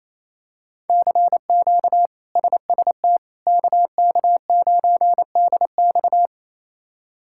Morse code, heard as a keyed tone, about 28 wpm, 700 Hz. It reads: CQ SST KK9DX
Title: K1USN SST QSO Practice - 28wpm